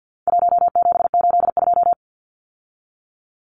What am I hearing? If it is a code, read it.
Morse code, heard as a keyed tone, about 48 wpm, 700 Hz. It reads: 1782
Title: Numbers - 4-Digit 50wpm